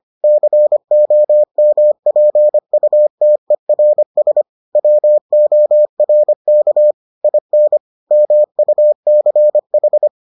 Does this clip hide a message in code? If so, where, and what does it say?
Yes: Morse code from 0.2 to 10.1 s, heard as a keyed tone, about 25 wpm, 600 Hz. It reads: COMPUTERS WORK IN MUCH